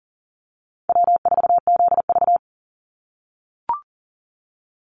Morse code, heard as a keyed tone, about 40 wpm, 700 Hz. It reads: W4ZV
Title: NA SOTA Chaser Callsigns 40wpm